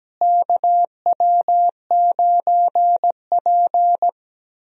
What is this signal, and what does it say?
Morse code, heard as a keyed tone, about 17 wpm, 700 Hz. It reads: KW9P
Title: USA Call Signs - 4 Characters - Repeated 3x - 17wpm